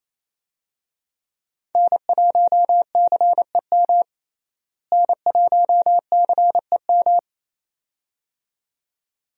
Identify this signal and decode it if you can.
Morse code, heard as a keyed tone, about 28 wpm, 700 Hz. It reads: N1CEM N1CEM
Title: USA Call Signs - 5 Characters - Repeated 2x - 28wpm